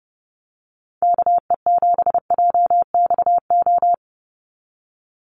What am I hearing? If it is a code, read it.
Morse code, heard as a keyed tone, about 30 wpm, 700 Hz. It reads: KE7JXO
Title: USA Call Signs 30wpm